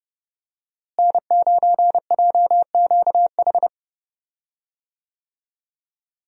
Morse code, heard as a keyed tone, about 30 wpm, 700 Hz. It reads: N9JQH
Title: USA Call Signs - 5 Characters - Repeated 2x - 30wpm